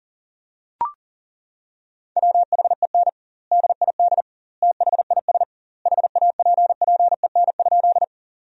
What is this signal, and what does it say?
Morse code, heard as a keyed tone, about 40 wpm, 700 Hz. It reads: WHEN DID THIS HAPPEN?
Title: 2x Word-Spacing - Sentences from Top 400 Words - 40wpm